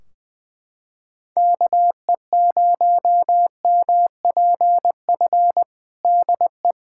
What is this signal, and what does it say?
Morse code, heard as a keyed tone, about 20 wpm, 700 Hz. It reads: KE0MPF DE
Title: Practice QSO - Exchange 4 - Ending - 20wpm